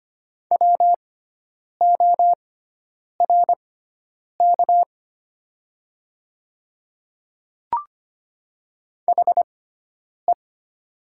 Morse code, heard as a keyed tone, about 25 wpm, spaced as for 9 wpm, 700 Hz. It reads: WORK HE
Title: 6x Character Spacing - Top 200 Words - 25wpm